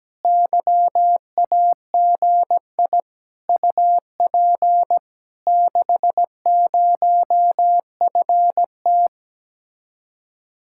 Morse code, heard as a keyed tone, about 17 wpm, 700 Hz. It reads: YAGI UP 60FT
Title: Speed-Racing Common QSO Phrases - 17wpm